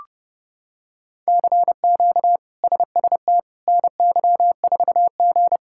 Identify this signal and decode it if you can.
Morse code, heard as a keyed tone, about 32 wpm, 700 Hz. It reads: CQ SST NY4G
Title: K1USN SST QSO Practice - 30wpm